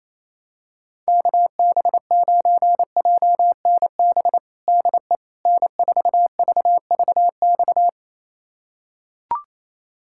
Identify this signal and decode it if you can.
Morse code, heard as a keyed tone, about 28 wpm, 700 Hz. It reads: KB9JNB DE N4VVX
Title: USA Callsign DE Callsign 28wpm